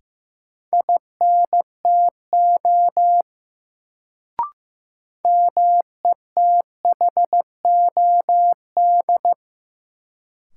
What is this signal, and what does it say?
Morse code, heard as a keyed tone, about 15 wpm, 700 Hz. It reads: INTO METHOD
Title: Sentences from Top 600 Words - New Word Drill - 15wpm